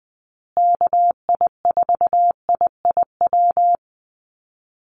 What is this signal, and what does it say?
Morse code, heard as a keyed tone, about 20 wpm, 700 Hz. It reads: KI4IIW
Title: USA Call Signs Repeated 3x 20wpm